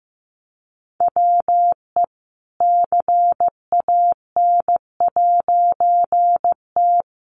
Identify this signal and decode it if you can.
Morse code, heard as a keyed tone, about 15 wpm, 700 Hz. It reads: WE CAN'T
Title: A Sound Of Thunder 15wpm